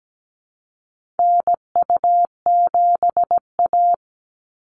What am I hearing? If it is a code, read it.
Morse code, heard as a keyed tone, about 17 wpm, 700 Hz. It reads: NU7A
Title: NA SOTA Chaser Callsigns 17wpm